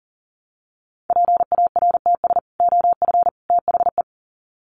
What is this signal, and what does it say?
Morse code, heard as a keyed tone, about 40 wpm, 700 Hz. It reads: PARTS OF THE
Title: Sets of 3 Words 40wpm